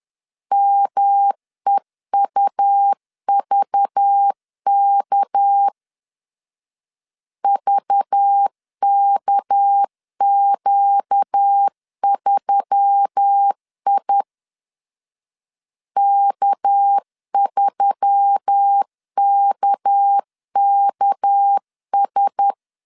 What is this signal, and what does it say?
Morse code, heard as a keyed tone, about 10 wpm, 800 Hz. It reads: MEUVK VKQ3I K3KKS
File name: LEZIONI / 6 PKX0 / 6 PK 10wpm 2436